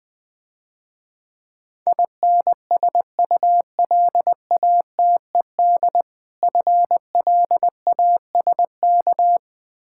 Morse code, heard as a keyed tone, about 20 wpm, 700 Hz. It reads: INSULATED FLASK
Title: Difficult Words - Speed-Racing with Hints - 20wpm